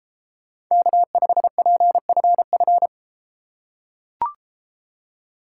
Morse code, heard as a keyed tone, about 33 wpm, 700 Hz. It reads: K5PFF